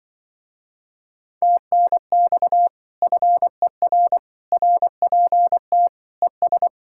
Morse code, heard as a keyed tone, about 24 wpm, 700 Hz. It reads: TNX FER RPT ES
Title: Speed-Racing Common QSO Phrases - 20wpm